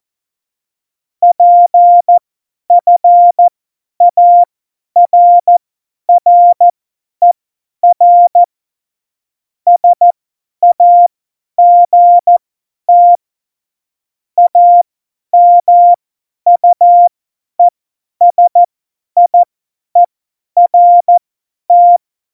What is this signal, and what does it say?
Morse code, heard as a keyed tone, about 14 wpm, spaced as for 10 wpm, 700 Hz. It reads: PFARRER SAGT AMUESIERT
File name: Morsekurs / Witze / Witz 08. Dezember 2020, 50 F70